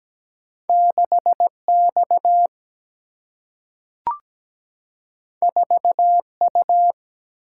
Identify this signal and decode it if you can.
Morse code, heard as a keyed tone, about 17 wpm, 700 Hz. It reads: 6X 4U
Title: Single-Letter Single-Number 17wpm